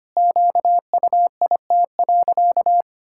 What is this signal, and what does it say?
Morse code, heard as a keyed tone, about 25 wpm, 700 Hz. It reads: QUIT.
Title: Pangram Challenge v2 - 25wpm